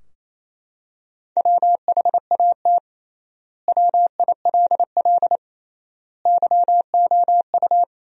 Morse code, heard as a keyed tone, about 28 wpm, 700 Hz. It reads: WHAT WILL YOU
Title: 3x Word-Spacing - Sentences from Top 200 Words - 28wpm